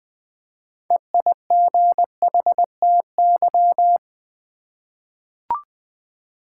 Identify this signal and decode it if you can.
Morse code, heard as a keyed tone, about 20 wpm, 700 Hz. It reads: EIGHTY
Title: 6 Letter Words - First 3 Letters Given - 20wpm